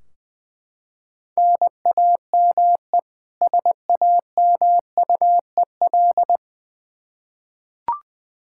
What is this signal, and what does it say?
Morse code, heard as a keyed tone, about 21 wpm, 700 Hz. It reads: NAME SAMUEL
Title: Practice QSO - Exchange 1 - Beginning - 20wpm